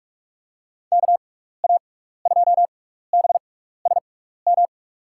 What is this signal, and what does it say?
Morse code, heard as a keyed tone, about 44 wpm, spaced as for 15 wpm, 700 Hz. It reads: KA2BSM
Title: USA Call Signs - 6 Characters - 6x Character Spacing - 45wpm